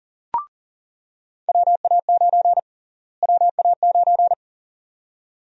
Morse code, heard as a keyed tone, about 40 wpm, 700 Hz. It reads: WA9 WA9
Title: USA Call Sign Prefixes - Repeated 2x - 40wpm